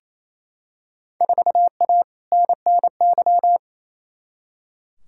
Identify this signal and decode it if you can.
Morse code, heard as a keyed tone, about 28 wpm, 700 Hz. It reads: 4A NNY
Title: Field Day Exchanges Encore - 28wpm